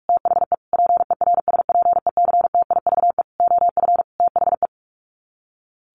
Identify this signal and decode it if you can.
Morse code, heard as a keyed tone, about 44 wpm, 700 Hz. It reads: THE PERSPECTIVE OF THE
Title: Sets of 5 Words Encore2 45wpm